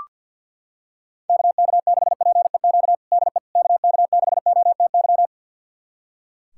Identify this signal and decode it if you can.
Morse code, heard as a keyed tone, about 45 wpm, 700 Hz. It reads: KK6PEQ DE KK6OTY
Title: USA Callsign DE Callsign - 6 Characters - 50wpm